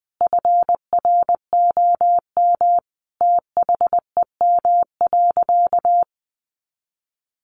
Morse code, heard as a keyed tone, about 20 wpm, 700 Hz. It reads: FROM THEM.
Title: Sentences from Top 900 Words 20wpm